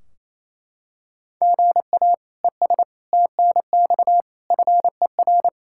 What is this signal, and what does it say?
Morse code, heard as a keyed tone, about 28 wpm, 700 Hz. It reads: GA ES TNX FER
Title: Practice QSO - Exchange 1 - Beginning - 28wpm